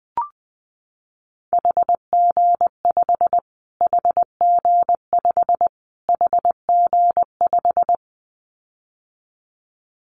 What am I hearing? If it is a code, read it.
Morse code, heard as a keyed tone, about 19 wpm, 700 Hz. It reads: HG5 HG5 HG5
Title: Callsign Prefixes for Top 25 DXCC - Repeated 3x - 20wpm